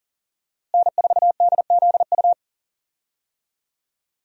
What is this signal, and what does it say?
Morse code, heard as a keyed tone, about 40 wpm, 700 Hz. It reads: N4DZU
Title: USA Call Signs - 5 Characters - Repeated 2x - 40wpm